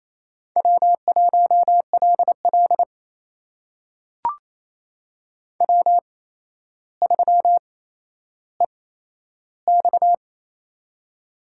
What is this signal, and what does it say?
Morse code, heard as a keyed tone, about 28 wpm, spaced as for 12 wpm, 700 Hz. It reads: W1LL W3EX
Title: USA Call Signs - 4 Characters - 8x Character Spacing - 28wpm